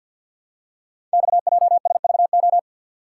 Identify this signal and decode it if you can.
Morse code, heard as a keyed tone, about 48 wpm, 700 Hz. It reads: XJSVO
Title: Code Groups - 5 Letters - 50wpm